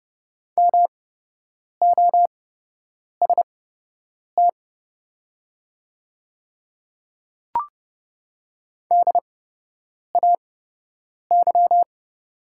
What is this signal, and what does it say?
Morse code, heard as a keyed tone, about 30 wpm, spaced as for 8 wpm, 700 Hz. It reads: MOST DAY
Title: 8x Character Spacing - Top 100 Words - 30wpm